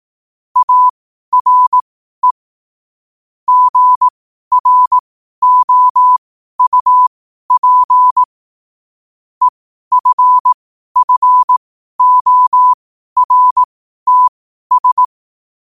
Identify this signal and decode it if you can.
Morse code, heard as a keyed tone, about 18 wpm, spaced as for 12 wpm, 1000 Hz. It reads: ARE GROUP EFFORTS
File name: qotdfiles / 12wpm